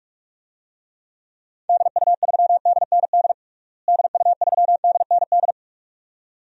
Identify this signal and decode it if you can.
Morse code, heard as a keyed tone, about 45 wpm, 700 Hz. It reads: DU3DND DU3DND
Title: DXCC-100 Call Signs - 6 Characters - Repeated 2x - 45wpm